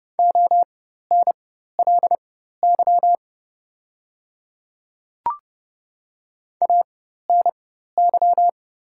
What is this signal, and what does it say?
Morse code, heard as a keyed tone, about 30 wpm, spaced as for 14 wpm, 700 Hz. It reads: ONLY ANY
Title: 4x Character Spacing - Top 200 Words - 30wpm